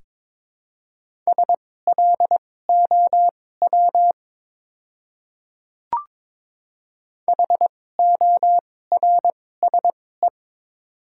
Morse code, heard as a keyed tone, about 21 wpm, spaced as for 16 wpm, 700 Hz. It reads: SLOW HORSE
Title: 2x Character Spacing - Top 500 Words - 22wpm